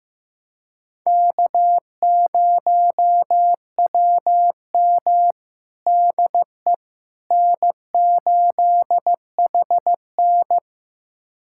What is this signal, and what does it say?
Morse code, heard as a keyed tone, about 15 wpm, 700 Hz. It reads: K0WM DE N8HN
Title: USA Callsign DE Callsign - 4 Characters - 15wpm